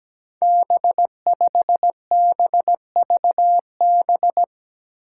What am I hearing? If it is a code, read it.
Morse code, heard as a keyed tone, about 17 wpm, 700 Hz. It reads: B5BVB
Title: Five Challenging Letter-Number Combinations 17wpm